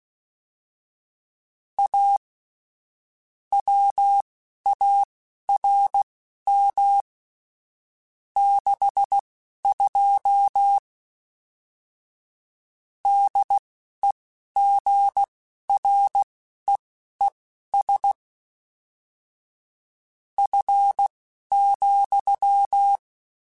Morse code, heard as a keyed tone, about 16 wpm, spaced as for 12 wpm, 800 Hz. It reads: A WARM 62 DEGREES F,
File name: Weather Sets / Weather Sets-mp3s / Weather 4 28 @10